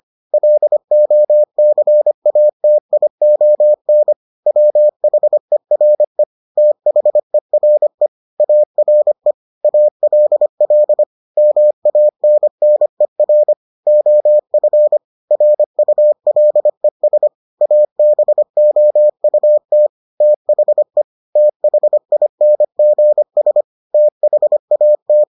Morse code, heard as a keyed tone, about 25 wpm, 600 Hz. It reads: LOCATION WHERE THERE ARE ALL MANNER OF RULES ABOUT THE THINGS THAT